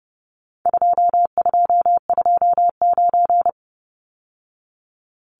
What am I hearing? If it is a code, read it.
Morse code, heard as a keyed tone, about 30 wpm, 700 Hz. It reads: 2229
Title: Numbers - 4-Digit 30wpm